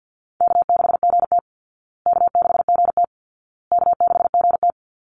Morse code, heard as a keyed tone, about 50 wpm, 700 Hz. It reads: K6GT K6GT K6GT
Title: Call Signs Repeated 2x - 50wpm